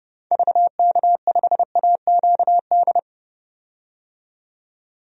Morse code, heard as a keyed tone, about 30 wpm, 700 Hz. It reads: VK5AQD
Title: DXCC-100 Call Signs - 6 Characters - 30wpm